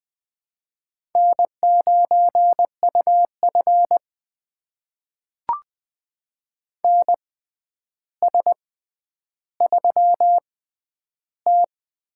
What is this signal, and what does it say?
Morse code, heard as a keyed tone, about 20 wpm, spaced as for 10 wpm, 700 Hz. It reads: N9UF NS3T
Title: USA Call Signs - 4 Characters - 6x Character Spacing - 20wpm